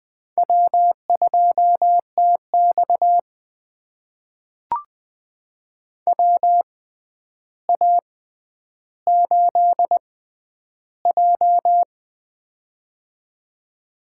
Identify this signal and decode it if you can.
Morse code, heard as a keyed tone, about 20 wpm, spaced as for 10 wpm, 700 Hz. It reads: W2TX WA8J
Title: USA Call Signs - 4 Characters - 6x Character Spacing - 20wpm